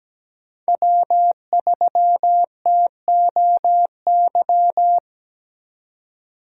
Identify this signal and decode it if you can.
Morse code, heard as a keyed tone, about 17 wpm, 700 Hz. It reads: W3TOY
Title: USA Call Signs - 2x Character Spacing - 17wpm